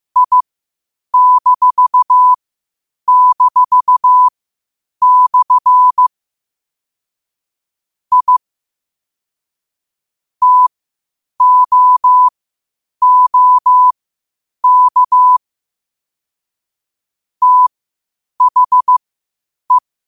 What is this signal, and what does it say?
Morse code, heard as a keyed tone, about 15 wpm, spaced as for 8 wpm, 1000 Hz. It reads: I--/ I TOOK THE
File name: qotdfiles / 8wpm